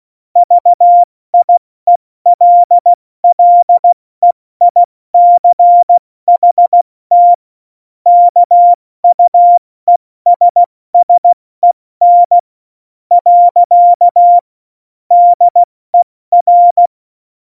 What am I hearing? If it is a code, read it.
Morse code, heard as a keyed tone, about 16 wpm, spaced as for 14 wpm, 700 Hz. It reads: VIELLEICHT KUESSEN. DER
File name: Morsekurs / Witze / Witz 29. Dezember 2020, 70 F80